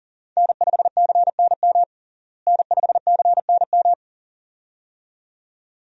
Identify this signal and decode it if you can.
Morse code, heard as a keyed tone, about 40 wpm, 700 Hz. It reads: N5CNM N5CNM
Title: USA Call Signs - 5 Characters - Repeated 2x - 40wpm